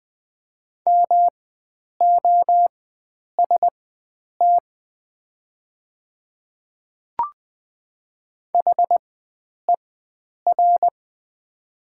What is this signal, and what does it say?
Morse code, heard as a keyed tone, about 20 wpm, spaced as for 9 wpm, 700 Hz. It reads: MOST HER
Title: 4x Character Spacing - Top 100 Words - 20wpm